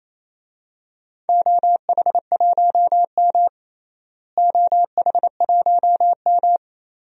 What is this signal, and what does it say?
Morse code, heard as a keyed tone, about 28 wpm, 700 Hz. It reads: OH1M OH1M
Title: DXCC-100 Call Signs - 4 Characters - Repeated 3x - 28wpm